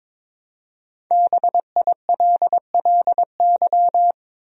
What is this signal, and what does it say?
Morse code, heard as a keyed tone, about 21 wpm, 700 Hz. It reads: BILLY
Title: USA Common Names - 22wpm